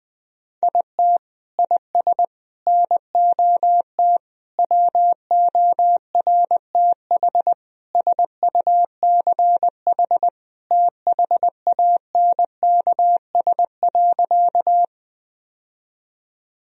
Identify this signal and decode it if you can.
Morse code, heard as a keyed tone, about 21 wpm, 700 Hz. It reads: IT IS NOT WORTH SUCH THANKS.
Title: Sentences from Top 1000 Words - Review Part 3 - 20wpm